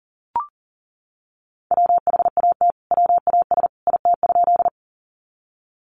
Morse code, heard as a keyed tone, about 40 wpm, 700 Hz. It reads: WHAT WAS IT?
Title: Sentences from Top 100 Words 40wpm